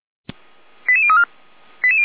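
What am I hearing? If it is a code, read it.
Five-tone selective call (ZVEI), heard as quick runs of five tones, tones of about 70 ms. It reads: 88034
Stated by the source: ZVEI five-tone selective call received with an SDR.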